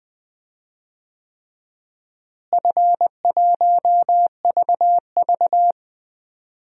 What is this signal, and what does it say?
Morse code, heard as a keyed tone, about 20 wpm, 700 Hz. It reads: F1VV
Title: DXCC-100 Call Signs - 4 Characters - Repeated 3x - 20wpm